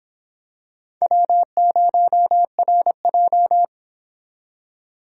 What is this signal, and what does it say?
Morse code, heard as a keyed tone, about 26 wpm, 700 Hz. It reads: W0RJ